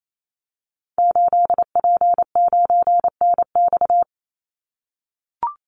Morse code, heard as a keyed tone, about 28 wpm, 700 Hz. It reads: 8P9NX
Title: Call Signs - 28wpm